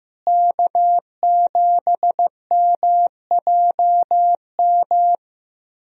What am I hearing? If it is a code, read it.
Morse code, heard as a keyed tone, about 15 wpm, 700 Hz. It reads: K7MJM
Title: USA Call Signs - 5 Characters - Repeated 3x - 15wpm